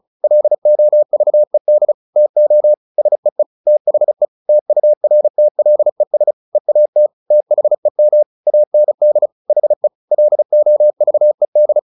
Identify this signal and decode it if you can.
Morse code, heard as a keyed tone, about 35 wpm, 600 Hz. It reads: LOVED TO SEE THE TURTLES EAT THEM AND HE LOVED
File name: the old man and the sea / the old man and the sea 1 35wpm